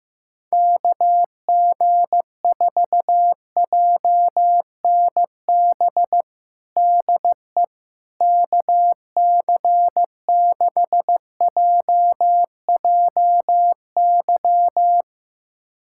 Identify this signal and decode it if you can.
Morse code, heard as a keyed tone, about 15 wpm, 700 Hz. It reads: KG4JNB DE KC6JJY
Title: USA Callsign DE Callsign 15wpm